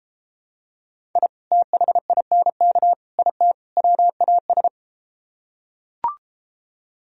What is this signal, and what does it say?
Morse code, heard as a keyed tone, about 33 wpm, 700 Hz. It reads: I THINK IT WAS